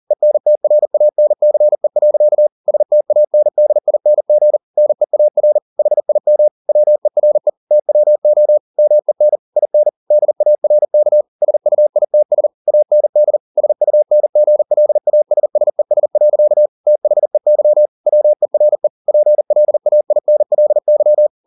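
Morse code, heard as a keyed tone, about 40 wpm, 600 Hz. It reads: ENTRANCE. STANDING NEAR HIM WERE TWO MEN IN DARK SUITS AND SUNGLASSES. THEY WERE PLAINLY
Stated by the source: Morse code at 40 wpm, practice text, CW med Gnister.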